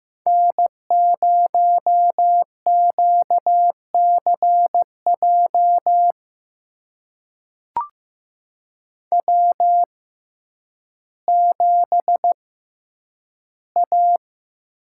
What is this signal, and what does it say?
Morse code, heard as a keyed tone, about 15 wpm, spaced as for 9 wpm, 700 Hz. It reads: N0QCJ W7A
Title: USA Call Signs - 5 Characters - 6x Character Spacing - 15wpm